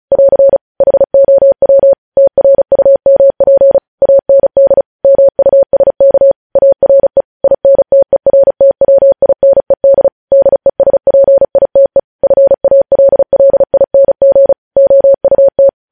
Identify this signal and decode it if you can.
Morse code, heard as a keyed tone, about 36 wpm, 550 Hz. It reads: <AR> HOW TRUMP AND MUSK ARE INTERTWINED DESPITE FALLING OUT
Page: News Headlines in Morse Code at 35 WPM